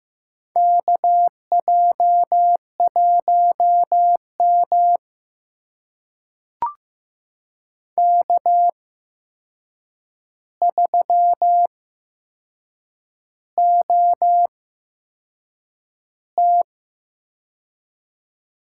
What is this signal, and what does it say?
Morse code, heard as a keyed tone, about 15 wpm, spaced as for 6 wpm, 700 Hz. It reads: KJ1M K3OT
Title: USA Call Signs - 4 Characters - 8x Character Spacing - 15wpm